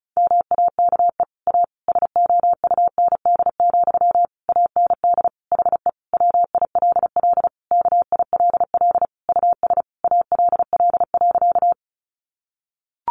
Morse code, heard as a keyed tone, about 34 wpm, 700 Hz. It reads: MAKE A SOUND, AND HE WILL KILL US ALL.
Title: Sentences from Top 1400 Words 35wpm